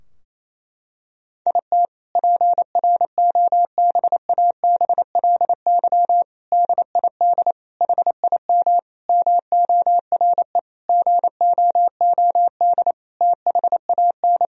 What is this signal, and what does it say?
Morse code, heard as a keyed tone, about 28 wpm, 700 Hz. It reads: IT PROBABLY DID HIM MORE GOOD THAN
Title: Sentences from Top 1000 Words - Review Part 4 - 28wpm